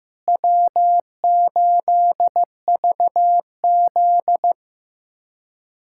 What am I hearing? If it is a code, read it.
Morse code, heard as a keyed tone, about 15 wpm, 700 Hz. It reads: W8VZ
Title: USA Callsign DE Callsign - 4 Characters - 15wpm